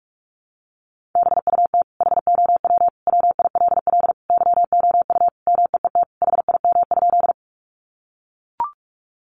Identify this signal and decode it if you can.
Morse code, heard as a keyed tone, about 44 wpm, 700 Hz. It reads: BUT HOW WILL YOU MEET HIM?
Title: Sentences from Top 1100 Words 45wpm